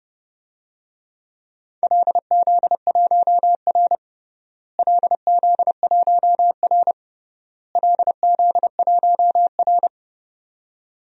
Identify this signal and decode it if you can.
Morse code, heard as a keyed tone, about 30 wpm, 700 Hz. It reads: LZ1R LZ1R LZ1R
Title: DXCC-100 Call Signs - 4 Characters - Repeated 3x - 30wpm